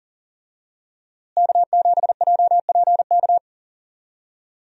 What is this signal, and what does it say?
Morse code, heard as a keyed tone, about 40 wpm, 700 Hz. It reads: K7JPK
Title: USA Call Signs - 5 Characters - Repeated 2x - 40wpm